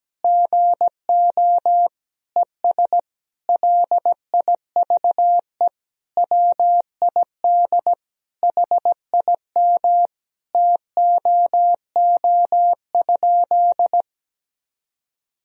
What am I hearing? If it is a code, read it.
Morse code, heard as a keyed tone, about 17 wpm, 700 Hz. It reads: GO ES LIVE WID HIM TOO?
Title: Sentences from Top 500 Words and Abbreviations 17wpm